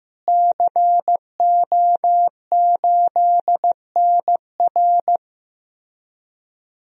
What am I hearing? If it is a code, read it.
Morse code, heard as a keyed tone, about 15 wpm, 700 Hz. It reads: CO8NR
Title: DXCC-100 Call Signs - 5 Characters - Repeated 3x - 15wpm